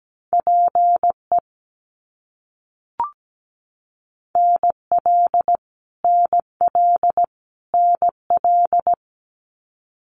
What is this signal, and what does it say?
Morse code, heard as a keyed tone, about 17 wpm, 700 Hz. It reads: PE NL NL NL
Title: Canadian Provinces and Territories - Repeated 3x - 17wpm